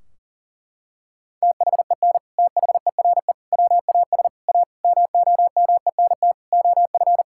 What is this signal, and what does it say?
Morse code, heard as a keyed tone, about 40 wpm, 700 Hz. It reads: THEN THERE WAS A MOMENT OF
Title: Sentences from Top 1600 Words 40wpm